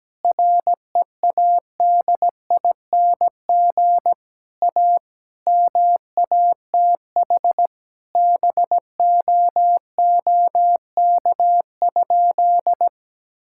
Morse code, heard as a keyed tone, about 17 wpm, 700 Hz. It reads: READING A MATH BOOK?
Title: Sentences from Top 1000 Words - Review Part 6 - 17wpm